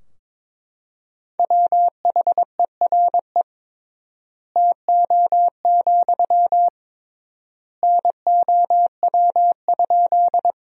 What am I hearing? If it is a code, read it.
Morse code, heard as a keyed tone, about 22 wpm, 700 Hz. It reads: WHERE TO, NOW?
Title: 3x Word-Spacing - Sentences from Top 200 Words - 22wpm